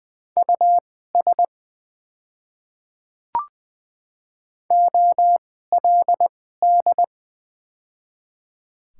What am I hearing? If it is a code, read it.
Morse code, heard as a keyed tone, about 20 wpm, spaced as for 14 wpm, 700 Hz. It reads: US OLD
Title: 2x Character Spacing - Top 200 Words - 20wpm